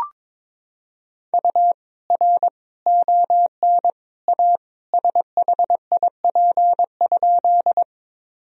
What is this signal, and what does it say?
Morse code, heard as a keyed tone, about 21 wpm, 700 Hz. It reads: U R ON A SHIP?
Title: Sentences from Top 400 Words and Abbreviations 22wpm